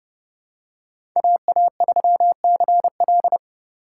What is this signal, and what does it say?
Morse code, heard as a keyed tone, about 30 wpm, 700 Hz. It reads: AA3CL
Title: USA Call Signs - 3x Character Spacing - 30wpm